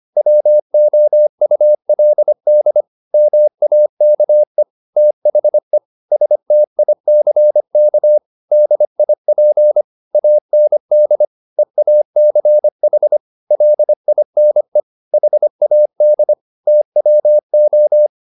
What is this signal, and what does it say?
Morse code, heard as a keyed tone, about 25 wpm, 600 Hz. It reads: WOULD MAKE THE STICK DIP AND EACH LINE HAD TWO
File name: the old man and the sea part 1 25wpm